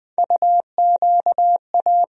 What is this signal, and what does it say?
Morse code, heard as a keyed tone, about 20 wpm, 700 Hz. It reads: UQA
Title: USA Call Sign Postfixes - 3 Letters 20wpm